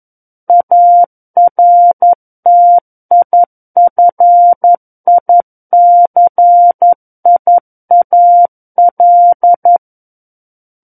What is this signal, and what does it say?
Morse code, heard as a keyed tone, about 11 wpm, 700 Hz. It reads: ARTIFICIAL
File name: Book One / 02-Chapter 2 - 11 WPM